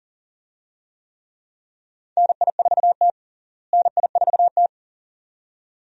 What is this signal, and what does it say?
Morse code, heard as a keyed tone, about 37 wpm, 700 Hz. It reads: NI4T NI4T
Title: USA Call Signs - 4 Characters - Repeated 2x - 40wpm